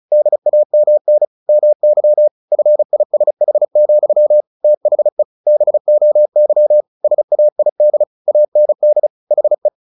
Morse code, heard as a keyed tone, about 34 wpm, 600 Hz. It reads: DAMN MY FISH, THE BOY SAID AND HE
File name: the old man and the sea 2 35wpm